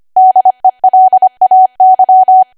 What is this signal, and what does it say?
Morse code, heard as a keyed tone, about 25 wpm, 750 Hz. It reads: DELAY
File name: Archive / 25 WPM / 25WPM